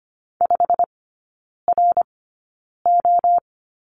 Morse code, heard as a keyed tone, about 25 wpm, spaced as for 9 wpm, 690 Hz. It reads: 5RO
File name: Lesson 15e - Number 5 - Three Characters 25wpm